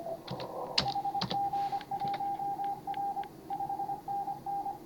Morse code, heard as a keyed tone, about 24 wpm, 810 Hz, with noise. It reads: HG1R 5NN
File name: CW / HA5MA RHOB HANG CW 2017